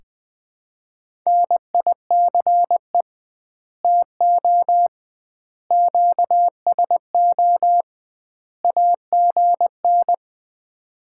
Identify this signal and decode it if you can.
Morse code, heard as a keyed tone, about 19 wpm, 700 Hz. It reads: NICE TO QSO AGN
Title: Common QSO Phrases - 2x Word-Spacing - 20wpm